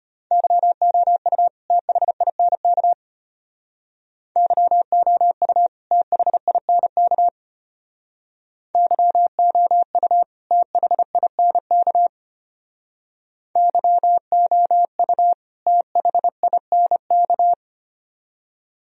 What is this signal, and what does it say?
Morse code, heard as a keyed tone, about 31 wpm, 700 Hz. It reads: YOU THINK YOU THINK YOU THINK YOU THINK